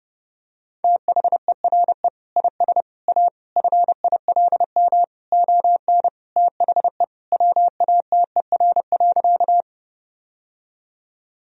Morse code, heard as a keyed tone, about 30 wpm, 700 Hz. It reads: THERE IS A FILM ON THE WATER.
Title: Sentences from Top 1000 Words - Review Part 3 - 30wpm